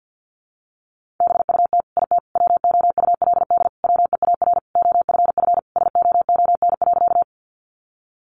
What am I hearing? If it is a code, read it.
Morse code, heard as a keyed tone, about 50 wpm, 700 Hz. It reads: BUT IT WOULD WEAR OFF SOON.
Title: Sentences from Top 1500 Words 50wpm